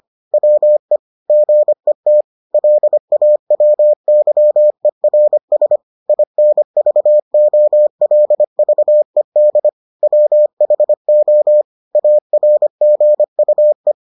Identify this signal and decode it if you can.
Morse code, heard as a keyed tone, about 25 wpm, 600 Hz. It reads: WE GET LAWYERS INVOLVED WHO ARGUE